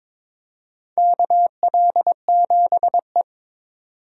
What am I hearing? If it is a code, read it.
Morse code, heard as a keyed tone, about 22 wpm, 700 Hz. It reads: KL7E
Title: DXCC-100 Call Signs - 4 Characters - 22wpm